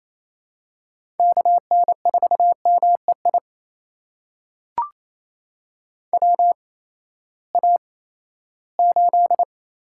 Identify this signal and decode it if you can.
Morse code, heard as a keyed tone, about 28 wpm, spaced as for 16 wpm, 700 Hz. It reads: KN4MEI WA8
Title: USA Call Signs - 6 Characters - 8x Character Spacing - 28wpm